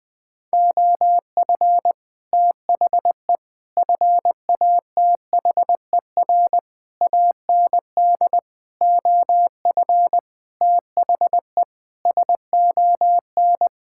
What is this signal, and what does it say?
Morse code, heard as a keyed tone, about 20 wpm, 700 Hz. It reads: OF THE FATHER AND OF THE SON
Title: Sentences with 9-Word Phrases - 20wpm